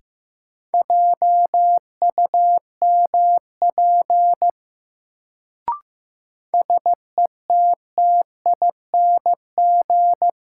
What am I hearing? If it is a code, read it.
Morse code, heard as a keyed tone, about 15 wpm, 700 Hz. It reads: JUMP SETTING
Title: Sentences from Top 1300 Words - New Word Drill - 15wpm